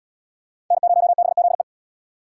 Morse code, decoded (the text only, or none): N0BZE